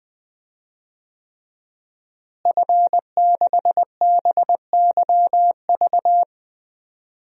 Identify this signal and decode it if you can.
Morse code, heard as a keyed tone, about 20 wpm, 700 Hz. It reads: F6BYV
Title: DXCC-100 Call Signs - 5 Characters - Repeated 3x - 20wpm